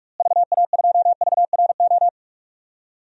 Morse code, heard as a keyed tone, about 46 wpm, 700 Hz. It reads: VA2VRO
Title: DXCC-100 Call Signs - 6 Characters - Repeated 2x - 45wpm